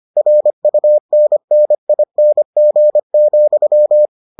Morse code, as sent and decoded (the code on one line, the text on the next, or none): .-. ..- -. -. .. -. --. --..--
RUNNING,